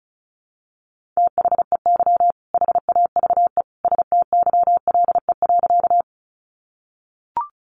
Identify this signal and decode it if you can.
Morse code, heard as a keyed tone, about 34 wpm, 700 Hz. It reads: THEY HAVE STYLE.
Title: Sentences from Top 800 Words 35wpm